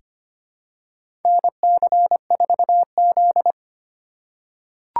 Morse code, heard as a keyed tone, about 25 wpm, 700 Hz. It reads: NC4Z
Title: USA Call Signs - 4 Characters - 3x Character Spacing - 25wpm